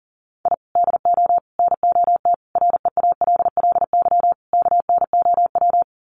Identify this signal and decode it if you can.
Morse code, heard as a keyed tone, about 40 wpm, 700 Hz. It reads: I DO NOT REALLY KNOW
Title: Sets of 5 Words Encore 40wpm